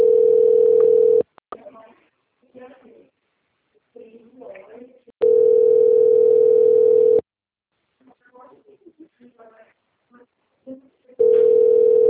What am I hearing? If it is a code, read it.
Ringback tone (North America), 440 + 480 Hz, with speech.
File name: out- - - - - .29400